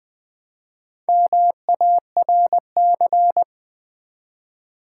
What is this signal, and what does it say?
Morse code, heard as a keyed tone, about 20 wpm, 690 Hz. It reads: MARC
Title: Build A Word - 1 Letter At A Time - 5-Letter Words - 20wpm